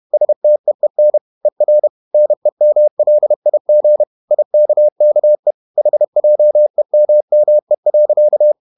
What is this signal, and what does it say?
Morse code, heard as a keyed tone, about 31 wpm, 600 Hz. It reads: STEEN ER NEMLIG IKKE HJEMME.
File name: Ferie QRQ 31wpm